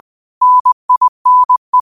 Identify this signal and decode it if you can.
Morse code, heard as a keyed tone, about 20 wpm, 1000 Hz. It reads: NINE
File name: qotdfiles / 20wpm